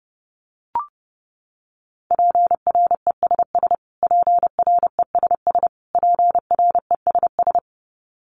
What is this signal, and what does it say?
Morse code, heard as a keyed tone, about 30 wpm, 700 Hz. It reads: PRESS PRESS PRESS
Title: Top 500 Words Repeated 30wpm